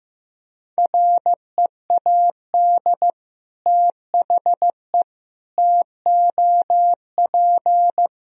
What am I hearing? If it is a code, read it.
Morse code, heard as a keyed tone, about 16 wpm, 700 Hz. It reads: READ THE TOP
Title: Sentences from Top 1000 Words - Review Part 2 - 15wpm